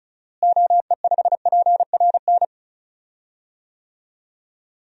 Morse code, heard as a keyed tone, about 34 wpm, 700 Hz. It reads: OE5PRN
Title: DXCC-100 Call Signs - 6 Characters - Repeated 3x - 35wpm